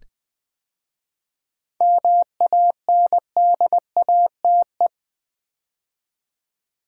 Morse code, heard as a keyed tone, about 20 wpm, 700 Hz. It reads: MANDATE
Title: 7 Letter Words - First 3 Letters Given - 20wpm